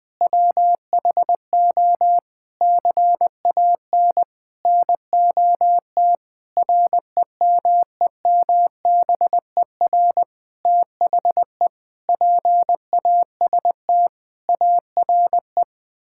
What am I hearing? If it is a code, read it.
Morse code, heard as a keyed tone, about 20 wpm, 700 Hz. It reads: WHO CAN NOT REMEMBER THE PAST ARE
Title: Sets of 8 Words - 20wpm